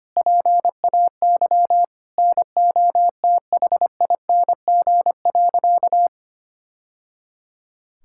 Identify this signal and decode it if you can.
Morse code, heard as a keyed tone, about 25 wpm, 700 Hz. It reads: PAY NOTHING.
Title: Sentences from Top 1000 Words - Review Part 5 - 25wpm